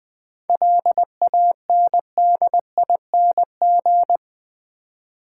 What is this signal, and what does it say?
Morse code, heard as a keyed tone, about 21 wpm, 700 Hz. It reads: LANDING
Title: 7 Letter Words - First 3 Letters Given - 20wpm